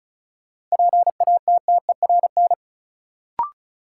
Morse code, heard as a keyed tone, about 36 wpm, 700 Hz. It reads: PATTERN
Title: Sentences from Top 1000 Words - New Word Drill - 35wpm